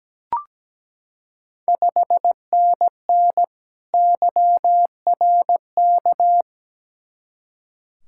Morse code, heard as a keyed tone, about 17 wpm, 700 Hz. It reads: 5NN YRK
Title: Ontario QSO Party Exchange 17wpm